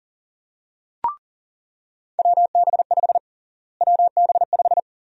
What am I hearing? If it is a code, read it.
Morse code, heard as a keyed tone, about 40 wpm, 700 Hz. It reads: WB5 WB5
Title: USA Call Sign Prefixes - Repeated 2x - 40wpm